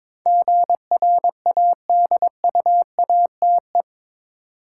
Morse code, heard as a keyed tone, about 22 wpm, 700 Hz. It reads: GRADUATE
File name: practice / 22wpm / Eight Letter Words - 22wpm